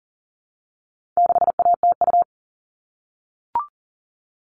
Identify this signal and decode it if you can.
Morse code, heard as a keyed tone, about 40 wpm, 700 Hz. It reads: 6ATU
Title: All Cut Numbers - 4-Digit 40wpm